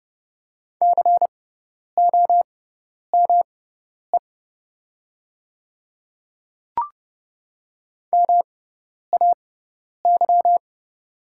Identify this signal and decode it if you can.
Morse code, heard as a keyed tone, about 30 wpm, spaced as for 10 wpm, 690 Hz. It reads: COME MAY
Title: 6x Character Spacing - Top 100 Words - 30wpm